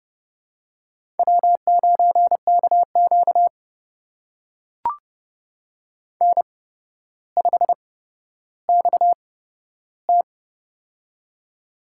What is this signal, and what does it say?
Morse code, heard as a keyed tone, about 31 wpm, spaced as for 13 wpm, 700 Hz. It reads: W9KQ N5XT
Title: USA Call Signs - 8x Character Spacing - 30wpm